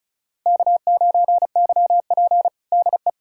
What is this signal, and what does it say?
Morse code, heard as a keyed tone, about 34 wpm, 700 Hz. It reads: K9YP DE
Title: USA Callsign DE Callsign - 4 Characters - 35wpm